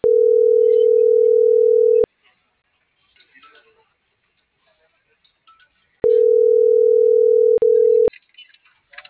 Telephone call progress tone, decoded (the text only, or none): ringback tone